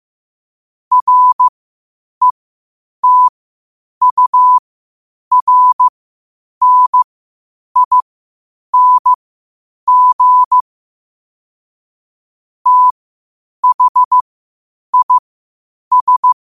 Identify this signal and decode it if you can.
Morse code, heard as a keyed tone, about 15 wpm, spaced as for 8 wpm, 1000 Hz. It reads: RETURNING THIS
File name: qotdfiles / 8wpm